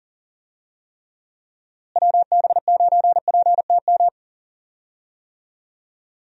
Morse code, heard as a keyed tone, about 40 wpm, 700 Hz. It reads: WB9PTM
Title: USA Call Signs - 6 Characters - Repeated 3x - 40wpm